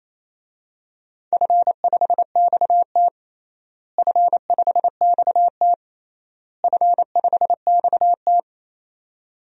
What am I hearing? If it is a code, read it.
Morse code, heard as a keyed tone, about 28 wpm, 700 Hz. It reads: F5XT F5XT F5XT
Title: DXCC-100 Call Signs - 4 Characters - Repeated 3x - 28wpm